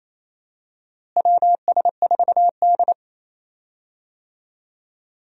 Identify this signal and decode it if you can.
Morse code, heard as a keyed tone, about 28 wpm, 700 Hz. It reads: WS4D
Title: USA Call Signs - 4 Characters - Repeated 2x - 28wpm